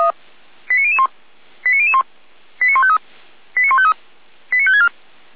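Five-tone selective call (ZVEI), heard as quick runs of five tones, tones of about 70 ms. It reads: 89001 89001 89153 89153 89564
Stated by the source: ZVEI five-tone call 89150 received by SDR.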